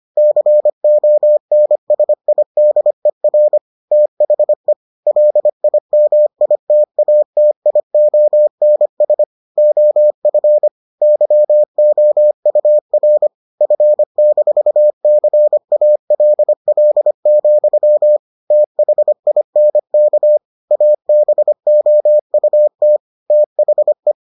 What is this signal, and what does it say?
Morse code, heard as a keyed tone, about 25 wpm, 600 Hz. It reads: CONSIDER THE LIMITATIONS OF YOUR F-CALL, THINK ABOUT THE